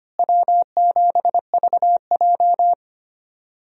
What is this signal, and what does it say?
Morse code, heard as a keyed tone, about 25 wpm, 700 Hz. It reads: W7VJ
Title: USA Call Signs - 4 Characters - Repeated 2x - 25wpm